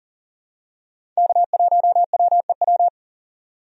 Morse code, heard as a keyed tone, about 40 wpm, 700 Hz. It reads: K1WEW
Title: USA Call Signs - 5 Characters - 40wpm